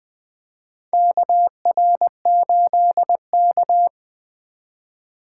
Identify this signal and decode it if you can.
Morse code, heard as a keyed tone, about 20 wpm, 700 Hz. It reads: KR8K
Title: USA Call Signs - 4 Characters - 6x Character Spacing - 20wpm